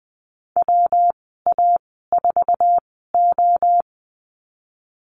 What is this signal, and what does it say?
Morse code, heard as a keyed tone, about 20 wpm, spaced as for 15 wpm, 700 Hz. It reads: WA4O
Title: 193 Lesson 31g - Letter v - Call Signs 20wpm